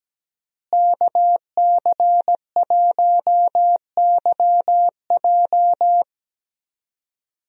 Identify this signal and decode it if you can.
Morse code, heard as a keyed tone, about 17 wpm, 700 Hz. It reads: KC1YJ
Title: USA Call Signs - 5 Characters - Repeated 3x - 17wpm